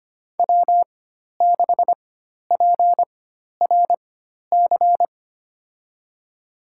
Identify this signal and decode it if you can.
Morse code, heard as a keyed tone, about 25 wpm, spaced as for 12 wpm, 700 Hz. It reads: W6PRC
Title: USA Call Signs - 4x Character Spacing - 25wpm